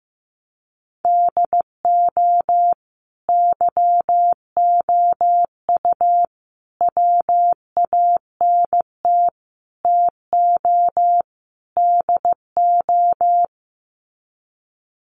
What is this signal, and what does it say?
Morse code, heard as a keyed tone, about 15 wpm, 700 Hz. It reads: DO YOU WANT TO DO
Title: Sets of 5 Words Encore 15wpm